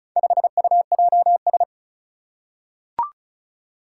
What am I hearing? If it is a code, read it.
Morse code, heard as a keyed tone, about 34 wpm, 700 Hz. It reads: 5UJS
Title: Four Challenging Letter-Number Combinations 35wpm